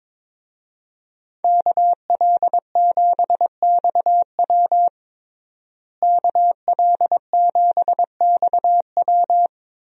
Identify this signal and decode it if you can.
Morse code, heard as a keyed tone, about 22 wpm, 700 Hz. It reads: KL7XW KL7XW
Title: DXCC-100 Call Signs - 5 Characters - Repeated 2x - 22wpm